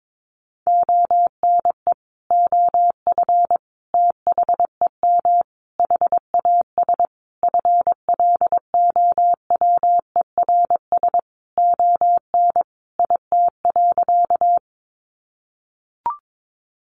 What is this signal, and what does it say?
Morse code, heard as a keyed tone, about 22 wpm, 700 Hz. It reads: ONE OF THEM HAS FLOWERS ON IT.
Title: Additional Easy Phrases 22wpm